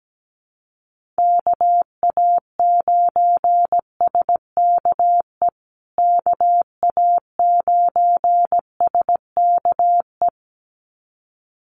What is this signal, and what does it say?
Morse code, heard as a keyed tone, about 17 wpm, 700 Hz. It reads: KA9SKE KA9SKE
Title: USA Call Signs Repeated 2x 17wpm